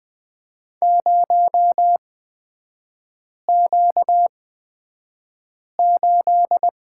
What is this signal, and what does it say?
Morse code, heard as a keyed tone, about 20 wpm, spaced as for 5 wpm, 700 Hz. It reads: 0Q8
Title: 236 Lesson 38e - Letter q - Three Characters 20wpm